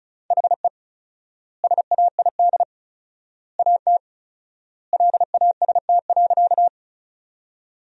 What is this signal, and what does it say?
Morse code, heard as a keyed tone, about 36 wpm, 700 Hz. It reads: HE SAID AT LAST.
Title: 4x Word-Spacing - Sentences from Top 300 Words - 35wpm